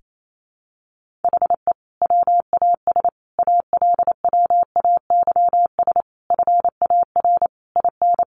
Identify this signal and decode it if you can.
Morse code, heard as a keyed tone, about 28 wpm, 700 Hz. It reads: HE WAS ALWAYS FAR IN
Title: Sentences from Top 1400 Words 28wpm